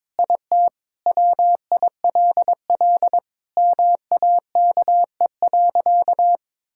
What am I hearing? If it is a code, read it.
Morse code, heard as a keyed tone, about 22 wpm, 700 Hz. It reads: IT WILL MAKE.
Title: Sentences from Top 1000 Words - Review Part 5 - 22wpm